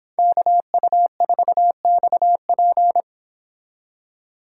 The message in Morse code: KU4XP

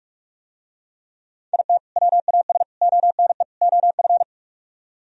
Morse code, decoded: IT WAS ONE OF

